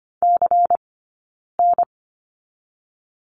Morse code, heard as a keyed tone, about 25 wpm, 700 Hz. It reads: C N
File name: course / 25:10wpm / 092 Lesson 16d - Letter c - Two Characters 25wpm